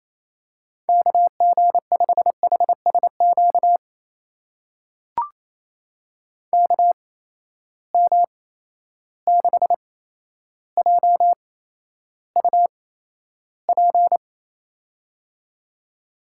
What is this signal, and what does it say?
Morse code, heard as a keyed tone, about 28 wpm, spaced as for 12 wpm, 700 Hz. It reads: KG5HSQ KM6JUP
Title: USA Call Signs - 6 Characters - 8x Character Spacing - 28wpm